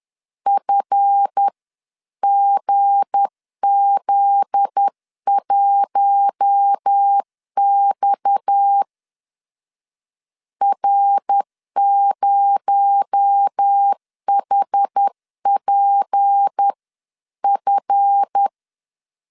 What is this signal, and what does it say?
Morse code, heard as a keyed tone, about 10 wpm, spaced as for 9 wpm, 800 Hz. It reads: FGZ1X R0HPF
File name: LEZIONI / 9 28 10wpm 2842